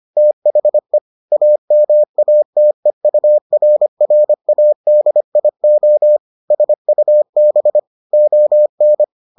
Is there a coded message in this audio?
Yes: Morse code from 0.2 to 9.0 s, heard as a keyed tone, about 25 wpm, 600 Hz. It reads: THE AMATEURRADIO SUB ON